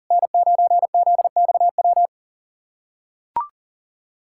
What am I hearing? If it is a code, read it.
Morse code, heard as a keyed tone, about 40 wpm, 700 Hz. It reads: N9ZXW
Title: SOTA QSO Practice 40wpm